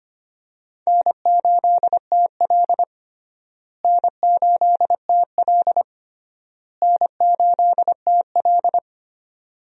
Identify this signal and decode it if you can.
Morse code, heard as a keyed tone, about 25 wpm, 700 Hz. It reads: N8TL N8TL N8TL
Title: USA Call Signs - 4 Characters - Repeated 3x - 25wpm